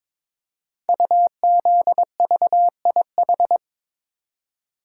Morse code, heard as a keyed tone, about 22 wpm, 700 Hz. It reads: UZVIH
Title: Code Groups - 5 Letters - 22wpm